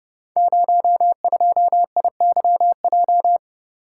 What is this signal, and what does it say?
Morse code, heard as a keyed tone, about 28 wpm, 700 Hz. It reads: 02IYJ
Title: Code Groups - 5 Letter:Numbers - 30wpm